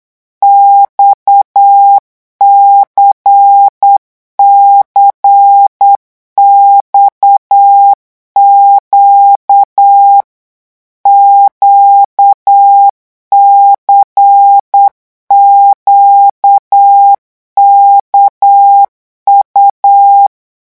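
Morse code, heard as a keyed tone, about 8 wpm, 800 Hz. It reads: XCCXQ QCQKU
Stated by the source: Morse code revision lesson at 8 words per minute, from a French amateur radio course.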